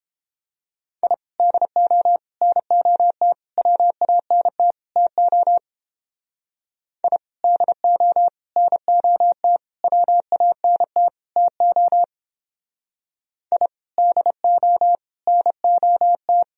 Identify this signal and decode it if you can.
Morse code, heard as a keyed tone, about 29 wpm, 700 Hz. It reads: I DO NOT WANT TO I DO NOT WANT TO I DO NOT